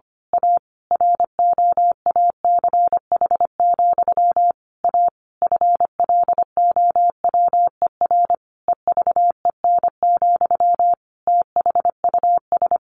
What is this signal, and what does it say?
Morse code, heard as a keyed tone, about 25 wpm, 700 Hz. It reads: A ROACH, A FLOWER EVEN, THUS
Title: A Sound Of Thunder 25wpm